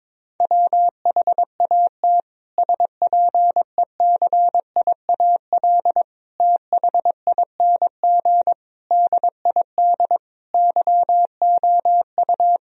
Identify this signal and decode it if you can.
Morse code, heard as a keyed tone, about 22 wpm, 700 Hz. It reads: WHAT SPECIAL THING DID YOU
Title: Sentences from Top 1000 Words - Review Part 3 - 22wpm